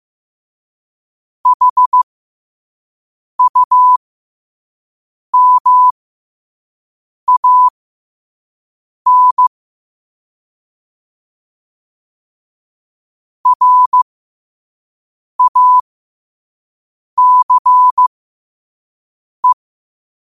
Morse code, heard as a keyed tone, about 15 wpm, spaced as for 5 wpm, 1000 Hz. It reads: HUMAN RACE
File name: qotdfiles / 5wpm